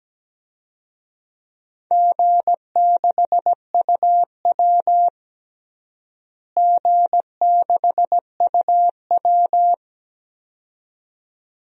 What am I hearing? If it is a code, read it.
Morse code, heard as a keyed tone, about 17 wpm, 700 Hz. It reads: G6UW G6UW
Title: DXCC-100 Call Signs - 4 Characters - Repeated 2x - 17wpm